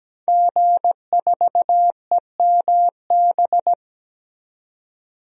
Morse code, heard as a keyed tone, about 17 wpm, 700 Hz. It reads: G4EMB
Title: DXCC-100 Call Signs - 5 Characters - 17wpm